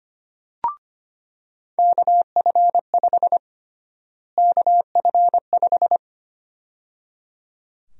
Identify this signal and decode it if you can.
Morse code, heard as a keyed tone, about 25 wpm, 700 Hz. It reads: KF5 KF5
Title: USA Call Sign Prefixes - Repeated 2x - 25wpm